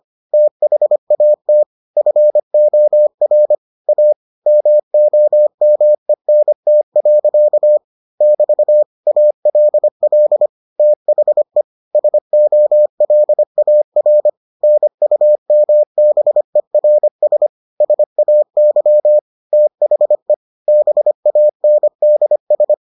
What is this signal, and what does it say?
Morse code, heard as a keyed tone, about 25 wpm, 600 Hz. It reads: THAT FOR A MOMENT. <BT> ALL THE SOLAR NUMBERS SAY THE BANDS